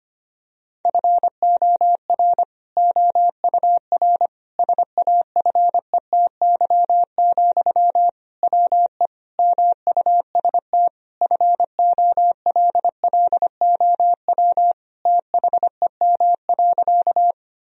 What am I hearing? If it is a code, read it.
Morse code, heard as a keyed tone, about 25 wpm, 700 Hz. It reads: FOR OUR SAFETY, WE MUST FOLLOW THEM.
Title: Sentences from Top 1000 Words - Review Part 4 - 25wpm